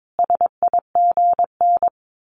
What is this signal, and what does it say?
Morse code, heard as a keyed tone, about 22 wpm, 700 Hz. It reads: SIGN
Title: A Sound Of Thunder 22wpm